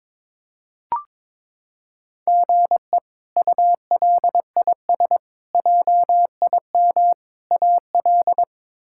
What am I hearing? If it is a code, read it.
Morse code, heard as a keyed tone, about 22 wpm, 700 Hz. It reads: GE ULIS JIM AL
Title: K1USN SST QSO Practice - 22wpm